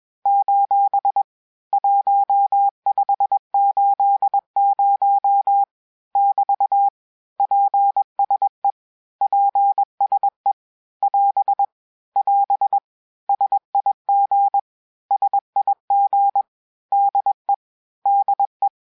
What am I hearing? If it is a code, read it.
Morse code, heard as a keyed tone, about 21 wpm, 800 Hz. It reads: : 1580 <BT> PSE PSE <AS> <AS> SIG SIG DE DE
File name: cours-du-lundi files / cours-1580-abreviations-mpm-20